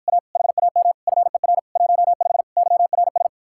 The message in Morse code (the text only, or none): A HAM FER 15 YRS